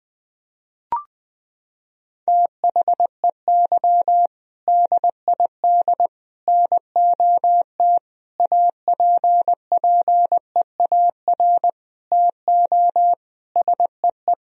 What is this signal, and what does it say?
Morse code, heard as a keyed tone, about 20 wpm, 700 Hz. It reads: THEY DID NOT APPEAR TO SEE
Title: Sentences from Top 1000 Words - Review Part 2 - 20wpm